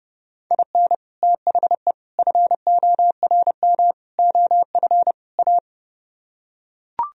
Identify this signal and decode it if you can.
Morse code, heard as a keyed tone, about 30 wpm, 700 Hz. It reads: IN THE FORM OF A